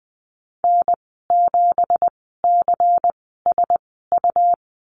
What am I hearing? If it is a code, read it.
Morse code, heard as a keyed tone, about 20 wpm, spaced as for 15 wpm, 700 Hz. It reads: N7CSU
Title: 179 Lesson 29g - Slash - Call Signs 20wpm